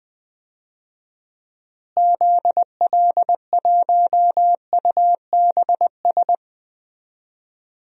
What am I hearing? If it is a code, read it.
Morse code, heard as a keyed tone, about 21 wpm, 700 Hz. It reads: ZL1UBS
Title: DXCC-100 Call Signs - 6 Characters - Repeated 3x - 20wpm